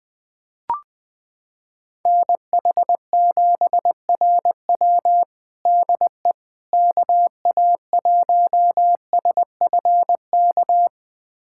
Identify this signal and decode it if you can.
Morse code, heard as a keyed tone, about 21 wpm, 700 Hz. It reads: NH7RW DE KA1SFK
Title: USA Callsign DE Callsign 20wpm